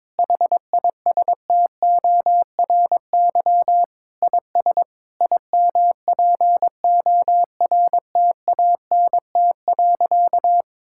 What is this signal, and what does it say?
Morse code, heard as a keyed tone, about 22 wpm, 700 Hz. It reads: HISTORY IS IMPORTANT.
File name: practice / 22wpm / Sentences with 8-Word Phrases - 22wpm